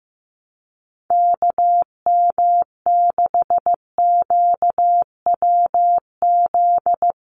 Morse code, heard as a keyed tone, about 15 wpm, 700 Hz. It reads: KM6QWZ
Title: USA Call Signs Repeated 2x 15wpm